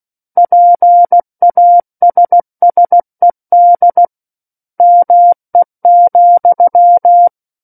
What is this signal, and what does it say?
Morse code, heard as a keyed tone, about 16 wpm, 700 Hz. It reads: PASSED ME,
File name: Book One / 07-Chapter 7 - 16 WPM